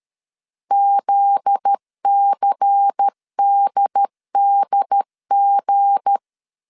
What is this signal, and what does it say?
Morse code, heard as a keyed tone, about 13 wpm, 800 Hz. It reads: ZCDDG
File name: CW / DET / 1 9 1500C S B 12wpm 3819